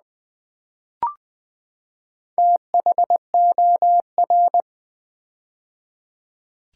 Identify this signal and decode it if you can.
Morse code, heard as a keyed tone, about 20 wpm, 700 Hz. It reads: THOR
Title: Build A Word - 4 Letters At A Time - 10-Letter Words - 20wpm